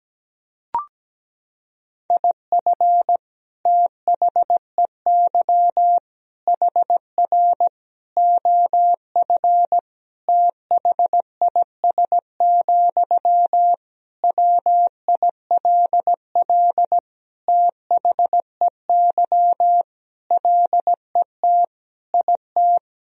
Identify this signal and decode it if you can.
Morse code, heard as a keyed tone, about 17 wpm, 700 Hz. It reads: IF THEY HR OF THIS, WILL THEY LET IT
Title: Sentences from Top 400 Words and Abbreviations 17wpm